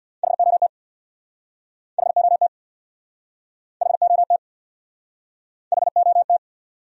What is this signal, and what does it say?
Morse code, heard as a keyed tone, about 68 wpm, 700 Hz. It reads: HOT HOT HOT HOT